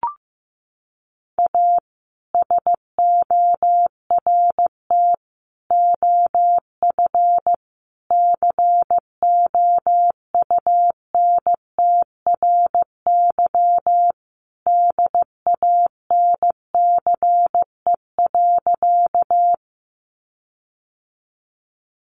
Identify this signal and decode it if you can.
Morse code, heard as a keyed tone, about 15 wpm, 700 Hz. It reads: A SORT OF COUNTRY DANCE.
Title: Sentences from Top 1300 Words 15wpm